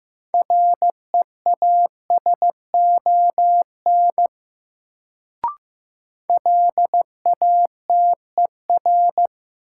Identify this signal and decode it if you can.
Morse code, heard as a keyed tone, about 15 wpm, 700 Hz. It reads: REASON LATER
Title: Sentences from Top 700 Words - New Word Drill - 15wpm